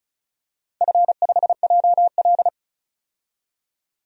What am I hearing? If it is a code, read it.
Morse code, heard as a keyed tone, about 36 wpm, 700 Hz. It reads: F5JL
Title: DXCC-100 Call Signs - 4 Characters - 35wpm